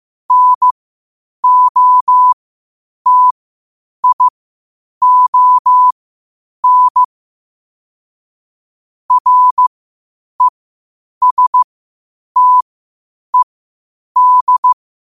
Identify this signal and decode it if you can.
Morse code, heard as a keyed tone, about 15 wpm, spaced as for 8 wpm, 1000 Hz. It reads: NOTION RESTED
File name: qotdfiles / 8wpm